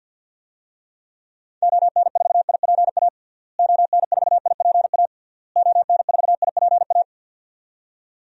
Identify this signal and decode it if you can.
Morse code, heard as a keyed tone, about 51 wpm, 700 Hz. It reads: ON4IPA ON4IPA ON4IPA
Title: DXCC-100 Call Signs - 6 Characters - Repeated 3x - 50wpm